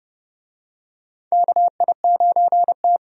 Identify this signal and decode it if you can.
Morse code, heard as a keyed tone, about 30 wpm, 700 Hz. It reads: KI9T
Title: USA Call Signs - 4 Characters - 30wpm